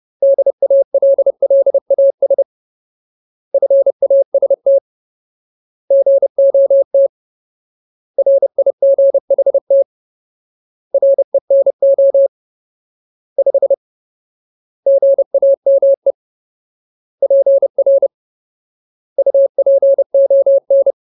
Morse code, heard as a keyed tone, about 30 wpm, 550 Hz. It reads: DALLAS FAST GOT RIGHT RENO 5 GAME PR UPON
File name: Ultimate Word List / 30WPM Ultimate